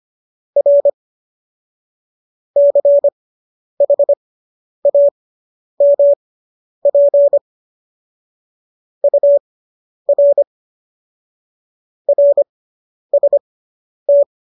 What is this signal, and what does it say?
Morse code, heard as a keyed tone, about 25 wpm, spaced as for 10 wpm, 570 Hz. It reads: R CHAMP UR RST